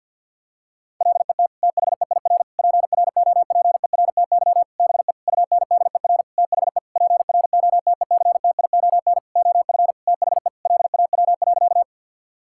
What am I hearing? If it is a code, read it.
Morse code, heard as a keyed tone, about 51 wpm, 700 Hz. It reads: LET THEIR PROPERTY BE UNDER THE PROTECTION OF THE LAW.
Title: Sentences from Top 1600 Words 50wpm